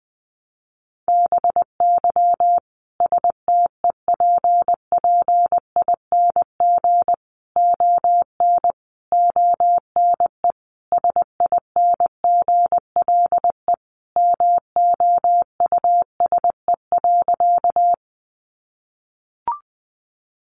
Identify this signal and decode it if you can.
Morse code, heard as a keyed tone, about 20 wpm, 700 Hz. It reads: BY STEPPING ON ONE SINGLE MOUSE.
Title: A Sound Of Thunder 20wpm